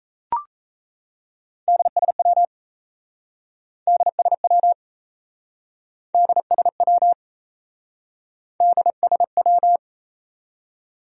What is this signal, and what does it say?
Morse code, heard as a keyed tone, about 38 wpm, 700 Hz. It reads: DSW DSW DSW DSW